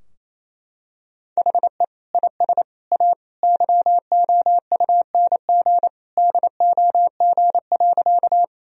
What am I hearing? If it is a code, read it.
Morse code, heard as a keyed tone, about 28 wpm, 700 Hz. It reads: HE IS A YOUNG DOG.
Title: Sentences from Top 1000 Words - Review Part 2 - 28wpm